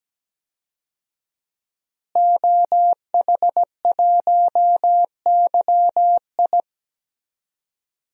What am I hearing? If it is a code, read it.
Morse code, heard as a keyed tone, about 17 wpm, 700 Hz. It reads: OH1YI
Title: DXCC-100 Call Signs - 5 Characters - Repeated 2x - 17wpm